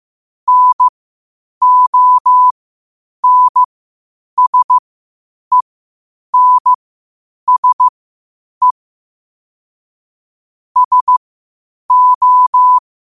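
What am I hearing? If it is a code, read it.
Morse code, heard as a keyed tone, about 15 wpm, spaced as for 8 wpm, 1000 Hz. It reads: NONSENSE SO
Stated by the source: Morse practice at 8 WPM from Morse Resource.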